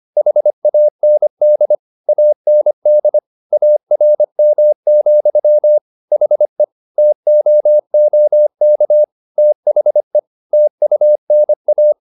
Morse code, heard as a keyed tone, about 25 wpm, 600 Hz. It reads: HAND AND ARM, HE TOOK THE TUNA
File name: the old man and the sea part 1 25wpm